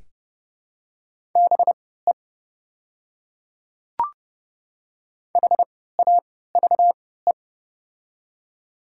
Morse code, heard as a keyed tone, about 30 wpm, spaced as for 17 wpm, 700 Hz. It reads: BE HAVE
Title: 3x Character Spacing - Top 200 Words - 30wpm